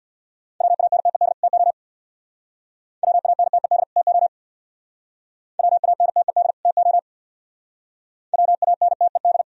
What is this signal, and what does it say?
Morse code, heard as a keyed tone, about 63 wpm, 700 Hz. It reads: WANTED TO WANTED TO WANTED TO WANTED